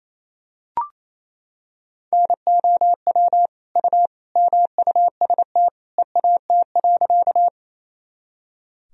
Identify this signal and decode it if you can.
Morse code, heard as a keyed tone, about 28 wpm, 700 Hz. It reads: NOW U MUST EAT.
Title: Sentences from Top 300 Words and Abbreviations 28wpm